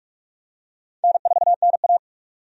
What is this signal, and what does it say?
Morse code, heard as a keyed tone, about 42 wpm, 700 Hz. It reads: N4NA
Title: USA Call Signs - 4 Characters - 45wpm